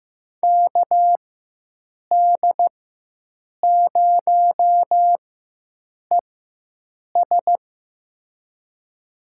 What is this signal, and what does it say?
Morse code, heard as a keyed tone, about 15 wpm, spaced as for 7 wpm, 700 Hz. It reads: KD0ES
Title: USA Call Signs - 5 Characters - 4x Character Spacing - 15wpm